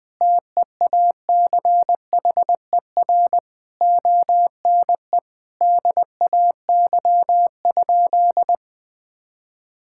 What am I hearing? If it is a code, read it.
Morse code, heard as a keyed tone, about 21 wpm, 700 Hz. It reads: TEACHER ONE DAY?
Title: Sentences from Top 1000 Words - Review Part 6 - 20wpm